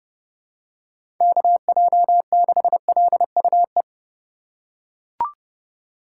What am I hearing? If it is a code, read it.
Morse code, heard as a keyed tone, about 31 wpm, 700 Hz. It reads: KJ6LUE
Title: USA Call Signs - 6 Characters - 4x Character Spacing - 30wpm